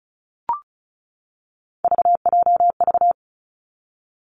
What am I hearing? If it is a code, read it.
Morse code, heard as a keyed tone, about 36 wpm, 700 Hz. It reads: VJV
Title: Three Challenging Letter-Number Combinations 35wpm